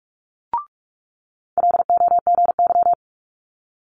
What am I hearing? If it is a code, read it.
Morse code, heard as a keyed tone, about 46 wpm, 700 Hz. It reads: LOGY
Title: Multi-letter Phonemes Quick Fire 45wpm